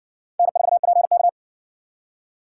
Morse code, heard as a keyed tone, about 60 wpm, 700 Hz. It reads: N4PK